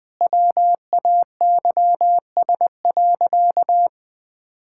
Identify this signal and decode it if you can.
Morse code, heard as a keyed tone, about 20 wpm, 700 Hz. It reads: WAYS.
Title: Sentences with 2-Word Phrases - 20wpm